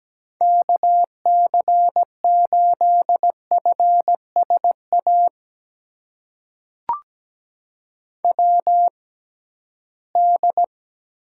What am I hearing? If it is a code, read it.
Morse code, heard as a keyed tone, about 17 wpm, spaced as for 13 wpm, 700 Hz. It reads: KC8FSA WD
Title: USA Call Signs - 6 Characters - 6x Character Spacing - 17wpm